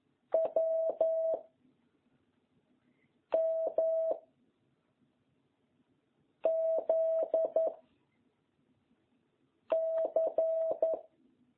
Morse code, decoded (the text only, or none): W M Z C